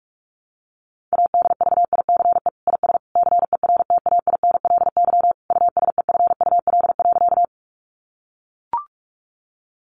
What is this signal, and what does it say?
Morse code, heard as a keyed tone, about 44 wpm, 700 Hz. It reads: ADVICE IS CERTAINLY USEFUL.
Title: Sentences from Top 1100 Words 45wpm